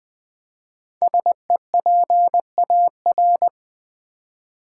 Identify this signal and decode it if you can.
Morse code, heard as a keyed tone, about 20 wpm, 700 Hz. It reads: SEPAR
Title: Build A Word - 5 Letters At A Time - 10-Letter Words - 20wpm